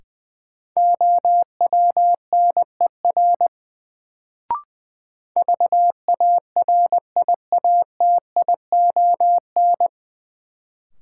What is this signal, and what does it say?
Morse code, heard as a keyed tone, about 21 wpm, 700 Hz. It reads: OWNER VARIATION
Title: Sentences from Top 1600 Words - New Word Drill Alt - 20wpm